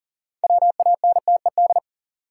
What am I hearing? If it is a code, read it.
Morse code, heard as a keyed tone, about 37 wpm, 700 Hz. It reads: WANTED
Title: Sets of 2 Words - New Word Drill - 40wpm